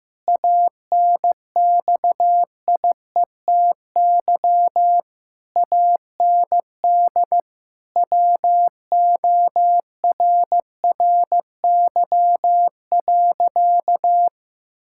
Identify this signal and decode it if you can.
Morse code, heard as a keyed tone, about 15 wpm, 700 Hz. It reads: ANXIETY AND WORRY.
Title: Pangram Challenge v2 - 15wpm